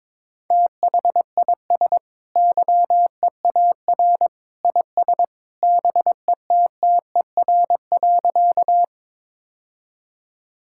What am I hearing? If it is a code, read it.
Morse code, heard as a keyed tone, about 22 wpm, 700 Hz. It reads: THIS YEAR IS BETTER.
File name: practice / 22wpm / Sentences with 2-Word Phrases - 22wpm